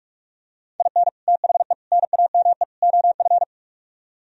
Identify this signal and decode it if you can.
Morse code, heard as a keyed tone, about 49 wpm, 700 Hz. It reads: IN THE NAME OF